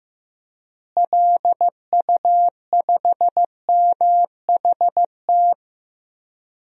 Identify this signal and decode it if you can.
Morse code, heard as a keyed tone, about 15 wpm, 700 Hz. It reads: LU5MHT
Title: DXCC-100 Call Signs - 6 Characters - Repeated 2x - 15wpm